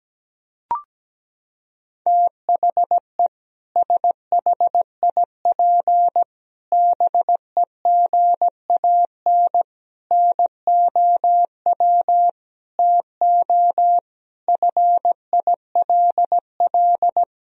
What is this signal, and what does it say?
Morse code, heard as a keyed tone, about 17 wpm, 700 Hz. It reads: THE SHIP BEGAN NOW TO FILL
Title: Sentences from Top 1000 Words - Review Part 3 - 17wpm